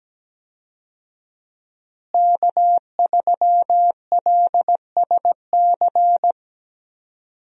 Morse code, heard as a keyed tone, about 17 wpm, 700 Hz. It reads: K3LSC
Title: USA Call Signs - 5 Characters - Repeated 3x - 17wpm